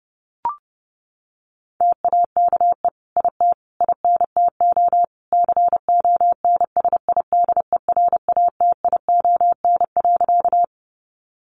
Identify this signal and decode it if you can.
Morse code, heard as a keyed tone, about 28 wpm, 700 Hz. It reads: TAKE IT INTO CONSIDERATION.
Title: Sentences from Top 1500 Words 30wpm